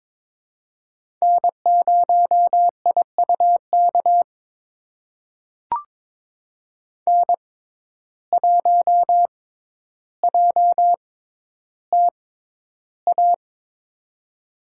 Morse code, heard as a keyed tone, about 22 wpm, spaced as for 11 wpm, 700 Hz. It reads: N0IUK N1JTA
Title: USA Call Signs - 5 Characters - 6x Character Spacing - 22wpm